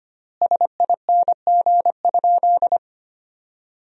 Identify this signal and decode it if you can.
Morse code, heard as a keyed tone, about 25 wpm, 700 Hz. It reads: SING?
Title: 3x Word-Spacing - Sentences from Top 400 Words - 25wpm